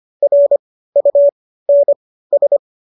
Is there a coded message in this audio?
Yes: Morse code from 0.2 to 2.6 s, heard as a keyed tone, about 25 wpm, spaced as for 15 wpm, 570 Hz. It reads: RUNS